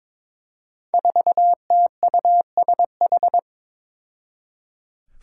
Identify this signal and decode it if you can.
Morse code, heard as a keyed tone, about 22 wpm, 700 Hz. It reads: 4TUSH
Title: Five Challenging Letter-Number Combinations 22wpm